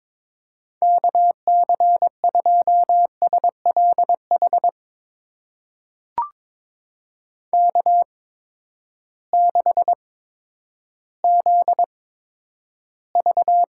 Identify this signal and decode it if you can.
Morse code, heard as a keyed tone, about 22 wpm, spaced as for 11 wpm, 700 Hz. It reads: KC2SLH K6ZV
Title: USA Call Signs - 8x Character Spacing - 22wpm